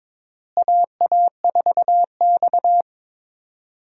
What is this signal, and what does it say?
Morse code, heard as a keyed tone, about 22 wpm, 700 Hz. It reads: AA4X
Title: USA Call Signs - 4 Characters - Repeated 3x - 22wpm